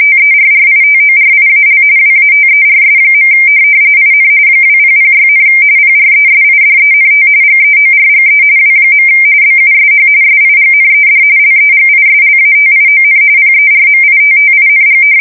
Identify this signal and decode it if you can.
RTTY, about 45.45 baud. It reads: TERRITORIAL WATERS. IN 2006, PRESIDENT GEORGE W. BUSH SIGNED LEGISLATION PERMITTING
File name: fsk170 / USS-New-Jersey